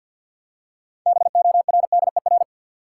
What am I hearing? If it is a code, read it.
Morse code, heard as a keyed tone, about 46 wpm, 700 Hz. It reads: BORDER
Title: Six Letter Words - 50wpm